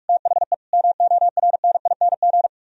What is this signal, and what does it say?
Morse code, heard as a keyed tone, about 48 wpm, 700 Hz. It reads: THE MORNING